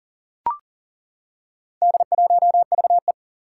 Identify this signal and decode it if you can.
Morse code, heard as a keyed tone, about 38 wpm, 700 Hz. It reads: D1VE
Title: Four Challenging Letter-Number Combinations 40wpm